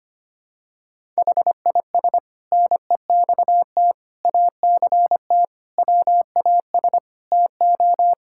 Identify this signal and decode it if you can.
Morse code, heard as a keyed tone, about 25 wpm, 700 Hz. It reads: HIS NEXT ACT WAS TO
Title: Sentences from Top 1000 Words - Review Part 5 - 25wpm